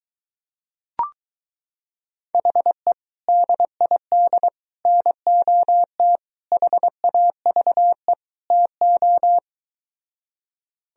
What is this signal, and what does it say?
Morse code, heard as a keyed tone, about 23 wpm, 700 Hz. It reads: HE DID NOT HAVE TO